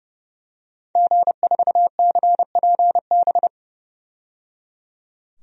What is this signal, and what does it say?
Morse code, heard as a keyed tone, about 30 wpm, 700 Hz. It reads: G4CPB
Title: DXCC-100 Call Signs - 5 Characters - 30wpm